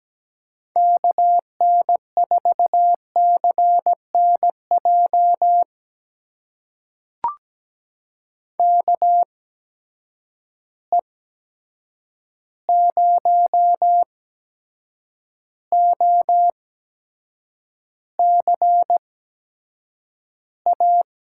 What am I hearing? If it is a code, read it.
Morse code, heard as a keyed tone, about 17 wpm, spaced as for 7 wpm, 700 Hz. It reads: KN4CNJ KE0OCA
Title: USA Call Signs - 6 Characters - 8x Character Spacing - 17wpm